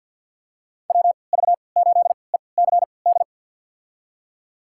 Morse code, heard as a keyed tone, about 48 wpm, spaced as for 28 wpm, 700 Hz. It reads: WV8ECD
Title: USA Call Signs - 6 Characters - 3x Character Spacing - 50wpm